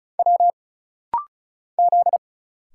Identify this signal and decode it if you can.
Morse code, heard as a keyed tone, about 36 wpm, 700 Hz. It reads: W Z
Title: Single Letters - Rapid-Fire 35wpm